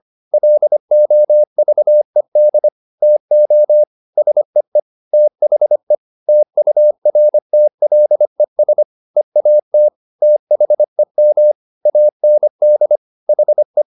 Morse code, heard as a keyed tone, about 25 wpm, 600 Hz. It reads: LOVED TO SEE THE TURTLES EAT THEM AND HE
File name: the old man and the sea part 1 25wpm